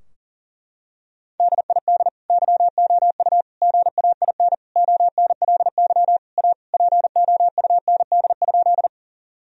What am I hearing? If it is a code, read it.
Morse code, heard as a keyed tone, about 40 wpm, 700 Hz. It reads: DID YOU GAIN ONLY A POUND?
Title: Sentences from Top 1000 Words - Review Part 2 - 40wpm